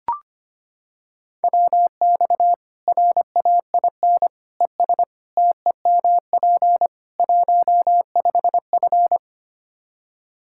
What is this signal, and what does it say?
Morse code, heard as a keyed tone, about 25 wpm, 700 Hz. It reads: WX RAIN ES TEMP 15F
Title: Practice QSO - Exchange 2 - Info - 25wpm